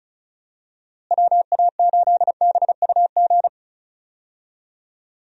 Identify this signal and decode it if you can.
Morse code, heard as a keyed tone, about 34 wpm, 700 Hz. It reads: WA8BUG
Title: USA Call Signs - 6 Characters - Repeated 2x - 35wpm